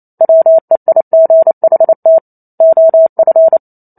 Morse code, heard as a keyed tone, about 28 wpm, 650 Hz. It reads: WEIGHT OF
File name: Chapter 2 - 28 WPM